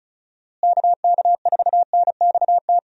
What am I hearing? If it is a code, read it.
Morse code, heard as a keyed tone, about 34 wpm, 700 Hz. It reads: KK4NXT
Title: USA Call Signs - 6 Characters - Repeated 2x - 35wpm